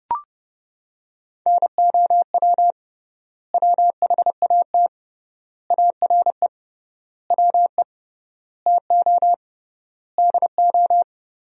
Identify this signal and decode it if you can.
Morse code, heard as a keyed tone, about 30 wpm, 700 Hz. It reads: NOW WHAT ARE WE TO DO
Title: 3x Word-Spacing - Sentences from Top 100 Words - 30wpm